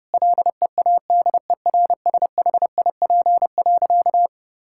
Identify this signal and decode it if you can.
Morse code, heard as a keyed tone, about 30 wpm, 700 Hz. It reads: LEADERSHIP.
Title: Sentences from Top 1600 Words 30wpm